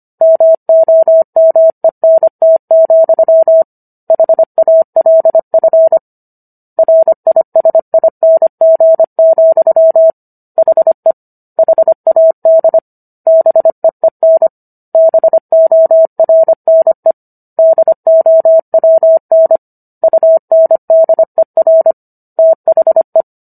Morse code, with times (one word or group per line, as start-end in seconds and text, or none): 0.2-3.6 MOMENT,
4.1-6.0 HALF
6.8-10.1 RISING,
10.6-11.1 HE
11.6-12.8 HAD
13.3-14.5 BEEN
14.9-17.1 BORNE
17.6-19.6 DOWN
20.0-21.9 UNDER
22.4-23.2 THE